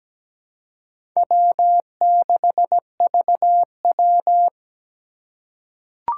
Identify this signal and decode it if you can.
Morse code, heard as a keyed tone, about 17 wpm, 700 Hz. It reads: W6VW
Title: USA Call Signs - 2x Character Spacing - 17wpm